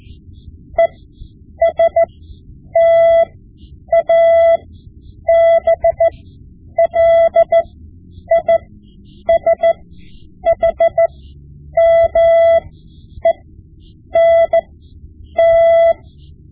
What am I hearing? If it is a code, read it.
Morse code, heard as a keyed tone, about 14 wpm, spaced as for 8 wpm, 670 Hz. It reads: ESTABLISHMENT
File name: Morse Lessons / Morse Lessons / Tape-5B